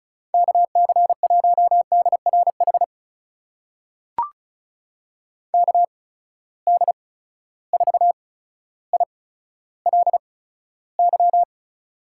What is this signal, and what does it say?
Morse code, heard as a keyed tone, about 36 wpm, 700 Hz. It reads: K C 1 D R H K D 4 I L Y
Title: USA Call Signs - 8x Character Spacing - 35wpm